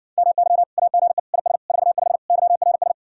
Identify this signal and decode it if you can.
Morse code, heard as a keyed tone, about 60 wpm, 700 Hz. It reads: MY AGE IS 45 YRS